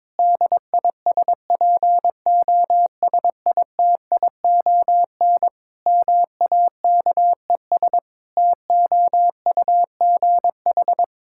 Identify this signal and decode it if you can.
Morse code, heard as a keyed tone, about 22 wpm, 700 Hz. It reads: DISPOSITION MAKES TOUGH
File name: Difficult Sentences with Hints - 22wpm